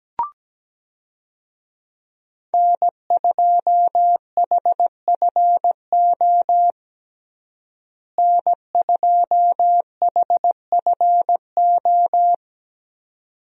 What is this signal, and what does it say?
Morse code, heard as a keyed tone, about 17 wpm, 700 Hz. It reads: N2HFO N2HFO
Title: USA Call Signs - 5 Characters - Repeated 2x - 17wpm